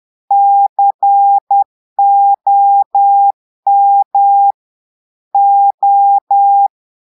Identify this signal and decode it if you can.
Morse code, heard as a keyed tone, about 10 wpm, 800 Hz. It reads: COM O
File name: portugais texte en clair 2 vitesse 10 mots minutes